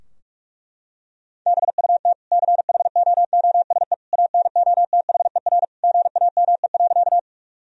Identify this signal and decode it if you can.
Morse code, heard as a keyed tone, about 46 wpm, 700 Hz. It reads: BUT CHOOSE ANOTHER GAME.
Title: Sentences from Top 1000 Words - Review Part 4 - 45wpm